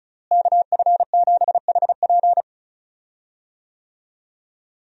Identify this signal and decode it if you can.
Morse code, heard as a keyed tone, about 35 wpm, 700 Hz. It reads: KF7HP
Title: USA Call Signs - 5 Characters - Repeated 3x - 35wpm